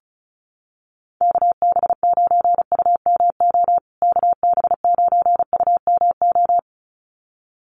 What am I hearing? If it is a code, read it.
Morse code, heard as a keyed tone, about 34 wpm, 700 Hz. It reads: KB9UMO KB9UMO
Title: USA Call Signs Repeated 2x 35wpm